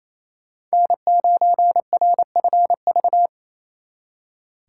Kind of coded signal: Morse code

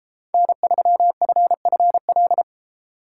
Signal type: Morse code

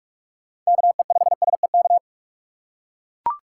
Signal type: Morse code

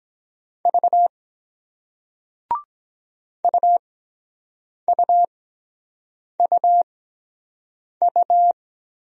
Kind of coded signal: Morse code

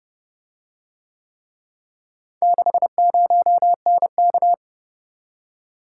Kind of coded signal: Morse code

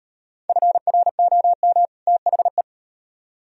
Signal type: Morse code

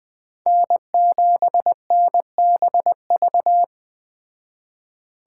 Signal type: Morse code

